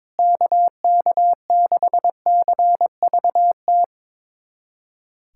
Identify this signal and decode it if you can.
Morse code, heard as a keyed tone, about 21 wpm, 700 Hz. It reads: KK6CVT
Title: USA Callsign DE Callsign 22wpm